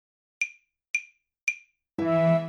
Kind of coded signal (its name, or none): none